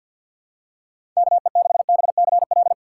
Morse code, heard as a keyed tone, about 52 wpm, 700 Hz. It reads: K E 6 B C L
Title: USA Call Signs - 6 Characters - 8x Character Spacing - 50wpm